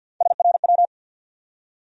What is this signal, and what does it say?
Morse code, heard as a keyed tone, about 50 wpm, 700 Hz. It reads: SRW
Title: USA Call Sign Postfixes - 3 Letters 50wpm